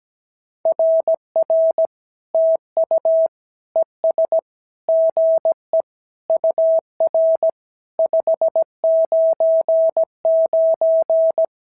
Morse code, heard as a keyed tone, about 17 wpm, 650 Hz. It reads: RR TU ES GE UR 599
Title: SOTA QSO Practice 17wpm